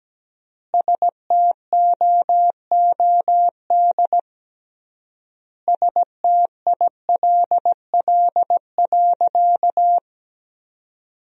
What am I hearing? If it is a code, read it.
Morse code, heard as a keyed tone, about 17 wpm, 700 Hz. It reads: STOOD STILL.
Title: 3x Word-Spacing - Sentences from Top 500 Words - 17wpm